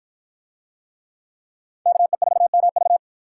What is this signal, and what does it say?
Morse code, heard as a keyed tone, about 53 wpm, 700 Hz. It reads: KE4MV